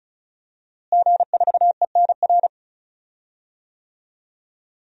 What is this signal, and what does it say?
Morse code, heard as a keyed tone, about 36 wpm, 700 Hz. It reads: G4ENR
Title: DXCC-100 Call Signs - 5 Characters - Repeated 3x - 35wpm